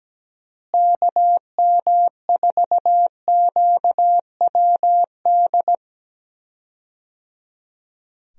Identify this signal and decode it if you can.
Morse code, heard as a keyed tone, about 17 wpm, 700 Hz. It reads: KM4QWD
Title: USA Call Signs - 6 Characters - Repeated 3x - 17wpm